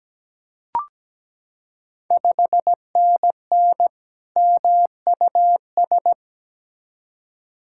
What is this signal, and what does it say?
Morse code, heard as a keyed tone, about 17 wpm, 690 Hz. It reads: 5NN MUS
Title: Ontario QSO Party Exchange 17wpm